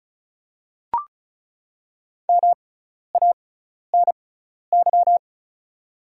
Morse code, heard as a keyed tone, about 34 wpm, spaced as for 12 wpm, 700 Hz. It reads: MANY
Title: 6x Character Spacing - Top 200 Words - 35wpm